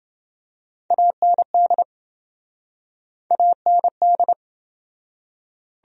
Morse code, heard as a keyed tone, about 27 wpm, 700 Hz. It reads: AND AND